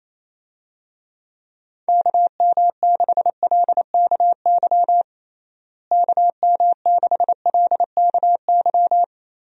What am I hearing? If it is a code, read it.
Morse code, heard as a keyed tone, about 28 wpm, 700 Hz. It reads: KM6LKY KM6LKY
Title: USA Call Signs - 6 Characters - Repeated 2x - 28wpm